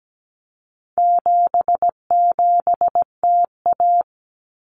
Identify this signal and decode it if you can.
Morse code, heard as a keyed tone, about 17 wpm, 700 Hz. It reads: 77TA
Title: Common Cut Numbers - 4-Digit 17wpm